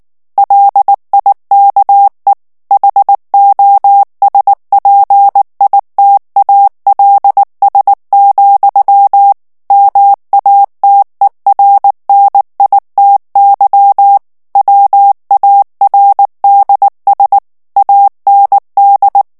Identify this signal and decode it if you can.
Morse code, heard as a keyed tone, about 19 wpm, 800 Hz. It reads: LIKE HOSPITALS, MATERNITY WARDS AND
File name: morse mp3 / ANGLAIS texte clair 2 vitesse 18 mots minute